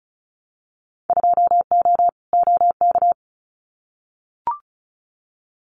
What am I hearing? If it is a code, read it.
Morse code, heard as a keyed tone, about 36 wpm, 700 Hz. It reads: 2O OK
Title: Winter Field Day Exchanges Encore 35wpm